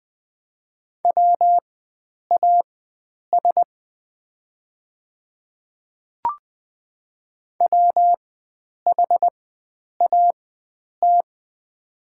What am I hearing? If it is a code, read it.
Morse code, heard as a keyed tone, about 20 wpm, spaced as for 9 wpm, 700 Hz. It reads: WAS WHAT
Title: 4x Character Spacing - Top 100 Words - 20wpm